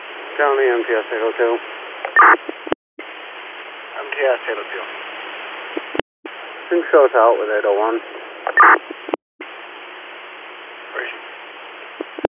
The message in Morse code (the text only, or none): none